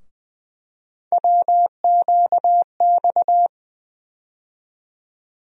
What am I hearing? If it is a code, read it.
Morse code, heard as a keyed tone, about 20 wpm, 700 Hz. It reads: WQX
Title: USA Call Sign Postfixes - 3 Letters 20wpm